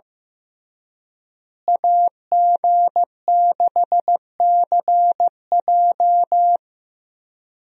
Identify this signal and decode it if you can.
Morse code, heard as a keyed tone, about 16 wpm, 700 Hz. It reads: AG6CJ
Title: USA Call Signs - 5 Characters - Repeated 3x - 15wpm